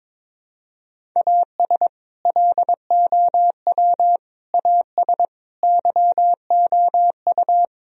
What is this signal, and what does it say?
Morse code, heard as a keyed tone, about 22 wpm, 700 Hz. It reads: AS LOW AS YOU
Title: Sentences from Top 1000 Words - Review Part 1 - 22wpm